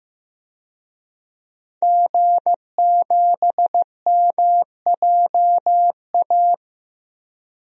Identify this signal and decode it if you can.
Morse code, heard as a keyed tone, about 15 wpm, 700 Hz. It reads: G7MJA
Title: DXCC-100 Call Signs - 5 Characters - Repeated 2x - 15wpm